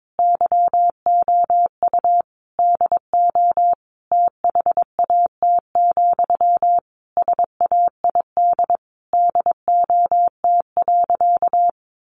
Morse code, heard as a keyed tone, about 22 wpm, 700 Hz. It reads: YOU DO THAT, SAID DOT.
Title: Sentences from Top 1500 Words 22wpm